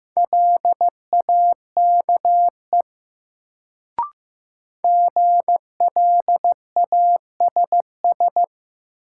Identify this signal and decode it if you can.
Morse code, heard as a keyed tone, about 16 wpm, 700 Hz. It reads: LAKE GLASS
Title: Sentences from Top 1000 Words - Review Part 3 - New Word Drill - 15wpm